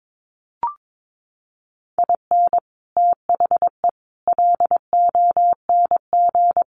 Morse code, heard as a keyed tone, about 22 wpm, 700 Hz. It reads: IN THE LONG
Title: Sets of 4 Words 22wpm